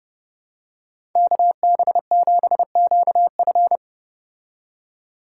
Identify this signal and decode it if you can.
Morse code, heard as a keyed tone, about 31 wpm, 700 Hz. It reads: KB7QF